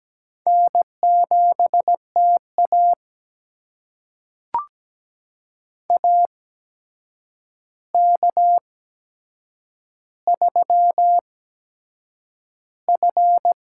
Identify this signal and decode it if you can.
Morse code, heard as a keyed tone, about 17 wpm, spaced as for 7 wpm, 700 Hz. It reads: N7TA AK3F
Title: USA Call Signs - 4 Characters - 8x Character Spacing - 17wpm